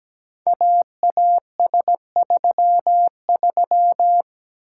Morse code, heard as a keyed tone, about 17 wpm, 700 Hz. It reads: AAS33
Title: Code Groups - 5 Letter:Numbers - 17wpm